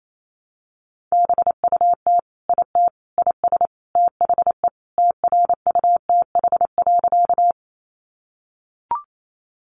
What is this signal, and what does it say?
Morse code, heard as a keyed tone, about 28 wpm, 700 Hz. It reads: BUT IT IS THE TRUTH.
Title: Sentences from Top 800 Words 28wpm